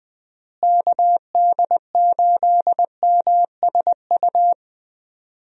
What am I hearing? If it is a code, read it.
Morse code, heard as a keyed tone, about 21 wpm, 700 Hz. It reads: KD8MSU
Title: USA Call Signs - 6 Characters - Repeated 3x - 20wpm